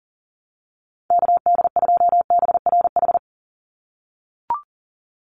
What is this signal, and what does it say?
Morse code, heard as a keyed tone, about 38 wpm, 700 Hz. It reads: KD2BRH
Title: USA Call Signs 40wpm